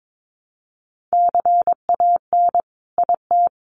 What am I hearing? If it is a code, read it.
Morse code, heard as a keyed tone, about 22 wpm, 700 Hz. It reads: CAN IT
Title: Sentences from Top 1300 Words 22wpm